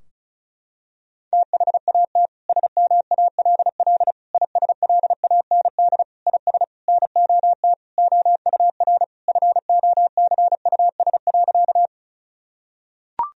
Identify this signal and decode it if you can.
Morse code, heard as a keyed tone, about 36 wpm, 700 Hz. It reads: THAT SMALL ISLAND IS NOT OUR FOCUS.
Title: Sentences from Top 1000 Words - Review Part 4 - 35wpm